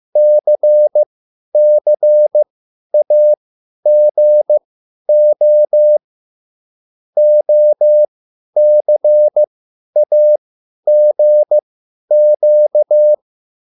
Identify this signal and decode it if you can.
Morse code, heard as a keyed tone, about 15 wpm, spaced as for 10 wpm, 600 Hz. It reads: CCAGO OCAGQ
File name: QSL / CW lesson